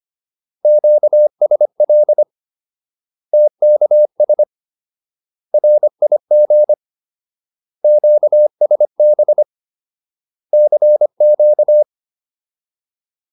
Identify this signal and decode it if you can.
Morse code, heard as a keyed tone, about 25 wpm, 600 Hz. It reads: QSL TKS RIG QSB CQ